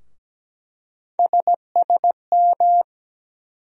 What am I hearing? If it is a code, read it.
Morse code, heard as a keyed tone, about 17 wpm, 700 Hz. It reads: SSM
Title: USA Call Sign Postfixes - 3 Letters 17wpm